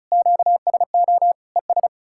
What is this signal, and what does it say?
Morse code, heard as a keyed tone, about 48 wpm, spaced as for 42 wpm, 700 Hz. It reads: QSO ES
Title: Speed-Racing Common QSO Phrases - 35wpm